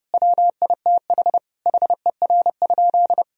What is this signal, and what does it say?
Morse code, heard as a keyed tone, about 30 wpm, 700 Hz. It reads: WITH HER?
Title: Sentences from Top 1000 Words - Review Part 3 - 30wpm